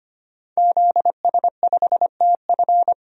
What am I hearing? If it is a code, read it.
Morse code, heard as a keyed tone, about 25 wpm, 700 Hz. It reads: ZS5TF
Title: DXCC-100 Call Signs - 5 Characters - 25wpm